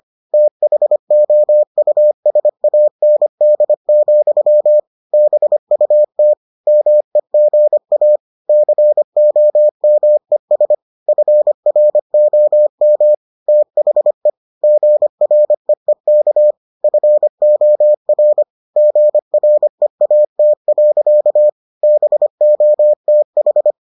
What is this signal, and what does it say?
Morse code, heard as a keyed tone, about 25 wpm, 600 Hz. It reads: THOUSAND, BUT MEGA COMES FROM THE GREEK FOR GREAT. BOTH